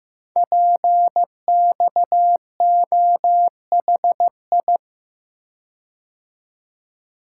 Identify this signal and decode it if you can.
Morse code, heard as a keyed tone, about 15 wpm, 700 Hz. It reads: PXOHI
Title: Code Groups - 5 Letters - 15wpm